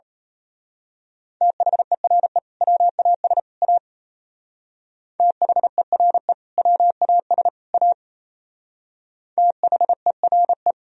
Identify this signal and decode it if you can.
Morse code, heard as a keyed tone, about 33 wpm, 700 Hz. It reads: THERE WAS A THERE WAS A THERE
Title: Speed-Racing - Sets of 3 Words 28wpm